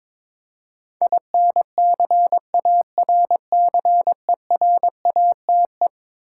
Morse code, heard as a keyed tone, about 22 wpm, 700 Hz. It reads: INCARCERATE
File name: practice / 22wpm / Eleven Letter Words - 22wpm